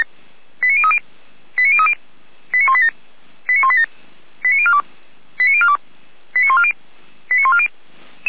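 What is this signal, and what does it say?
Five-tone selective call (ZVEI), heard as quick runs of five tones, tones of about 70 ms. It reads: 89030 89030 89187 89187 89042 89042 89140 89140
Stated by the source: ZVEI five-tone selective call 89299, received by SDR.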